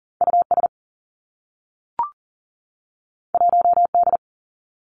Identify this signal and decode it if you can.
Morse code, heard as a keyed tone, about 39 wpm, 700 Hz. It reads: U S 1 D
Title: Two Challenging Letter-Number Combinations 40wpm